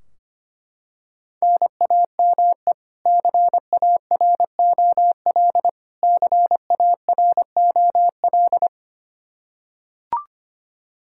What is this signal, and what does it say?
Morse code, heard as a keyed tone, about 25 wpm, 700 Hz. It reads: NAME CAROL CAROL
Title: Practice QSO - Exchange 1 - Beginning - 25wpm